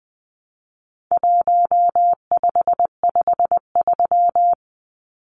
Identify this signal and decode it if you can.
Morse code, heard as a keyed tone, about 20 wpm, 700 Hz. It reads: 1553
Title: Numbers - 4-Digit 20wpm